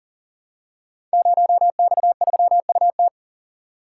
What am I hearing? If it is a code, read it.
Morse code, heard as a keyed tone, about 40 wpm, 700 Hz. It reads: 0X3UT
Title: Code Groups - 5 Letter:Numbers - 40wpm